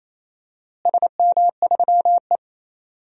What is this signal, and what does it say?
Morse code, heard as a keyed tone, about 28 wpm, 700 Hz. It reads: SM3E
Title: DXCC-100 Call Signs - 4 Characters - Repeated 3x - 28wpm